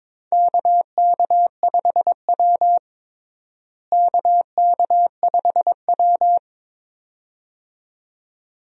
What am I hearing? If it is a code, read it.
Morse code, heard as a keyed tone, about 21 wpm, 700 Hz. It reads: KK5W KK5W
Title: USA Call Signs - 4 Characters - Repeated 2x - 22wpm